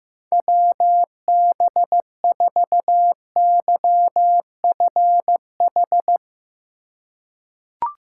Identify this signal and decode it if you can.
Morse code, heard as a keyed tone, about 15 wpm, 700 Hz. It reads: WB4YFH
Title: USA Call Signs - 2x Character Spacing - 15wpm